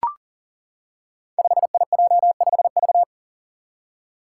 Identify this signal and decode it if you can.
Morse code, heard as a keyed tone, about 38 wpm, 700 Hz. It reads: 5IJ5V
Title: Five Challenging Letter-Number Combinations 40wpm